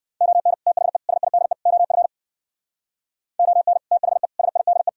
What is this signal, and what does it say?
Morse code, heard as a keyed tone, about 60 wpm, 700 Hz. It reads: ON THE SIDE OF ON THE SIDE